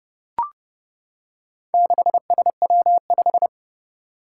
Morse code, heard as a keyed tone, about 30 wpm, 700 Hz. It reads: 6SW5
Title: Four Challenging Letter-Number Combinations 30wpm